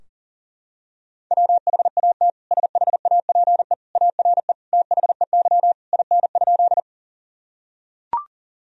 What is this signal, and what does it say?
Morse code, heard as a keyed tone, about 40 wpm, 700 Hz. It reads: WHAT SHAPE ARE THEY IN?
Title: Sentences from Top 1000 Words - Review Part 3 - 40wpm